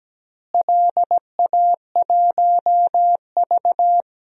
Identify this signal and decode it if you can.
Morse code, heard as a keyed tone, about 17 wpm, 700 Hz. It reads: LA1V
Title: DXCC-100 Call Signs - 4 Characters - Repeated 2x - 17wpm